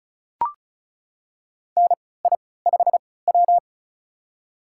Morse code, heard as a keyed tone, about 36 wpm, spaced as for 20 wpm, 700 Hz. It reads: NI5W
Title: USA Call Signs - 3x Character Spacing - 35wpm